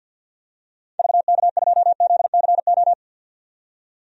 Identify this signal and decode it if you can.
Morse code, heard as a keyed tone, about 53 wpm, 700 Hz. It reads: VK2ZCO
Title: DXCC-100 Call Signs - 6 Characters - 50wpm